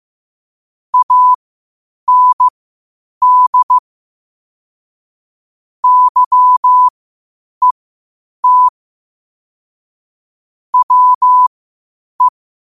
Morse code, heard as a keyed tone, about 15 wpm, spaced as for 8 wpm, 1000 Hz. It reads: AND YET WE
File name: qotdfiles / 8wpm